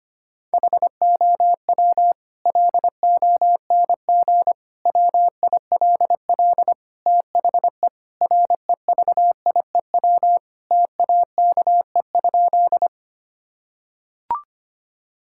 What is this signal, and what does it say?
Morse code, heard as a keyed tone, about 25 wpm, 700 Hz. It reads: HOW LONG WILL THE REVIEW TAKE?
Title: Sentences from Top 1000 Words - Review Part 6 - 25wpm